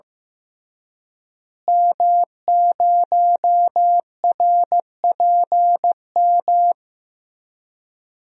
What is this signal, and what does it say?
Morse code, heard as a keyed tone, about 15 wpm, 700 Hz. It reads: M0RPM
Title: DXCC-100 Call Signs - 5 Characters - Repeated 3x - 15wpm